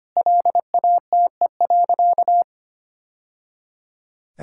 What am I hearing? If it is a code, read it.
Morse code, heard as a keyed tone, about 25 wpm, 700 Hz. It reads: LATE.
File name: practice / 25wpm / Sentences with 3-Word Phrases - 25wpm